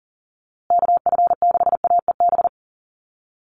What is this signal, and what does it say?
Morse code, heard as a keyed tone, about 40 wpm, 700 Hz. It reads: KF6AEB
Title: USA Call Signs Repeated 3x 40wpm